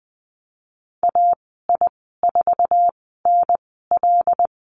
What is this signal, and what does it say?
Morse code, heard as a keyed tone, about 20 wpm, spaced as for 15 wpm, 700 Hz. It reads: AI4NL
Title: 088 Lesson 15g - Number 5 - Call Signs 20wpm